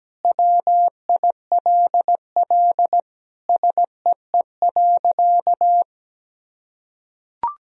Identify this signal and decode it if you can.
Morse code, heard as a keyed tone, about 17 wpm, 700 Hz. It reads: WILL SEE.
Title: Sentences from Top 1000 Words - Review Part 1 - 17wpm